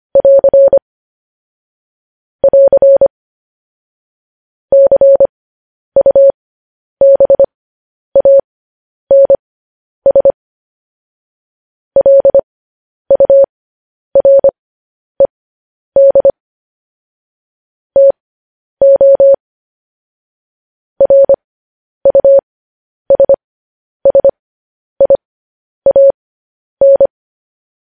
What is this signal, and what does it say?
Morse code, heard as a keyed tone, about 25 wpm, spaced as for 10 wpm, 550 Hz. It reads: <AR> <AR> CUBANS LURED TO RUSSIAN